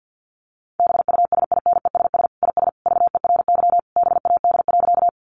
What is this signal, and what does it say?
Morse code, heard as a keyed tone, about 50 wpm, 700 Hz. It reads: BUSINESS IS VERY BAD.
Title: Sentences from Top 700 Words 50wpm